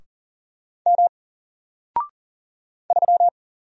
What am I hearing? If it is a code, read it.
Morse code, heard as a keyed tone, about 40 wpm, 700 Hz. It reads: M 3
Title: Single Character (Letter:Number:Punctuation:Prosign) Alt 40wpm